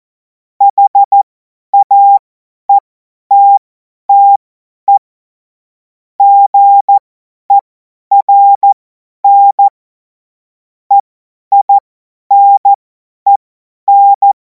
Morse code, heard as a keyed tone, about 14 wpm, spaced as for 10 wpm, 800 Hz. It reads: HAETTE GERN EINEN
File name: Morsekurs / Witze / Witz 17. Februar 2026 50 F70